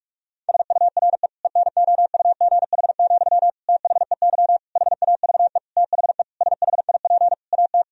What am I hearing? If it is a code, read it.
Morse code, heard as a keyed tone, about 46 wpm, 700 Hz. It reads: SURE ENOUGH, THEY HAVE THE SHIP AT
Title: Sentences from Top 1000 Words - Review Part 2 - 45wpm